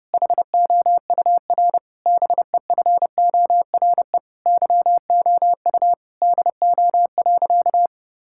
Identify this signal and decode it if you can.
Morse code, heard as a keyed tone, about 30 wpm, 700 Hz. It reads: HOUR BEFORE YOU DO.
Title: Sentences from Top 1000 Words - Review Part 6 - 30wpm